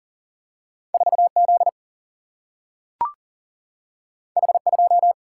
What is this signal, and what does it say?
Morse code, heard as a keyed tone, about 40 wpm, 700 Hz. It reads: 4 Z H 2
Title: Single-Letter Single-Number 40wpm